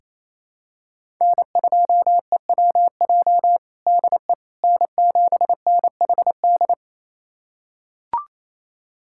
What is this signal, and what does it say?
Morse code, heard as a keyed tone, about 27 wpm, 700 Hz. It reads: N2EWJ DE N7NHD
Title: USA Callsign DE Callsign - 5 Characters - 28wpm